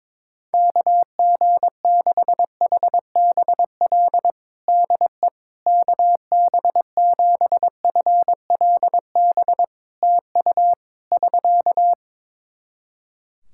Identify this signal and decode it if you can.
Morse code, heard as a keyed tone, about 22 wpm, 700 Hz. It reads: KG6HBL DE KB7FLB TU <SK>
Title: Practice QSO - Exchange 4 - Ending - 22wpm